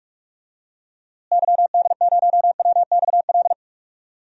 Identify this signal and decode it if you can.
Morse code, heard as a keyed tone, about 42 wpm, 700 Hz. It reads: YD0WXL
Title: DXCC-100 Call Signs - 6 Characters - Repeated 2x - 45wpm